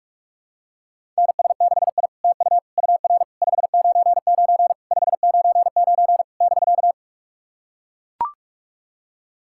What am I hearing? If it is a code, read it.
Morse code, heard as a keyed tone, about 44 wpm, 690 Hz. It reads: NS6I TU UR 599 599 <BK>
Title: SOTA QSO Practice 45wpm